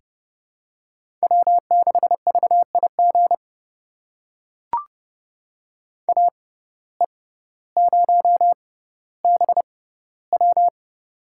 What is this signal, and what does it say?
Morse code, heard as a keyed tone, about 32 wpm, spaced as for 16 wpm, 700 Hz. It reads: W6VIG AE0BW
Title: USA Call Signs - 5 Characters - 6x Character Spacing - 30wpm